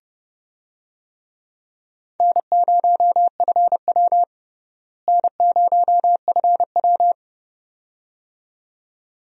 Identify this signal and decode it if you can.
Morse code, heard as a keyed tone, about 32 wpm, 700 Hz. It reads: N0FW N0FW
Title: USA Call Signs - 4 Characters - Repeated 2x - 30wpm